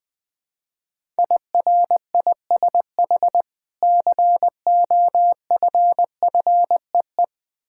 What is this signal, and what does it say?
Morse code, heard as a keyed tone, about 20 wpm, 700 Hz. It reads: IRISH COFFEE
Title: Difficult Words with Hints - 20wpm